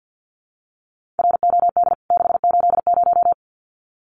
Morse code, heard as a keyed tone, about 50 wpm, 700 Hz. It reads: ROD 680
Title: CWT Exchange 50wpm